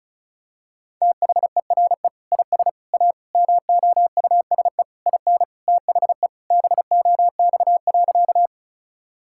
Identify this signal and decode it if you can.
Morse code, heard as a keyed tone, about 34 wpm, 700 Hz. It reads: THERE IS A MOUSE IN THE BOX.
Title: Sentences from Top 1000 Words - Review Part 5 - 35wpm